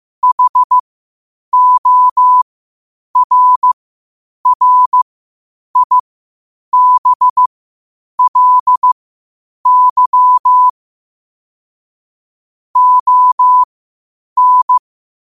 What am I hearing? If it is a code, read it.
Morse code, heard as a keyed tone, about 15 wpm, spaced as for 8 wpm, 1000 Hz. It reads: HORRIBLY ON
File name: qotdfiles / 8wpm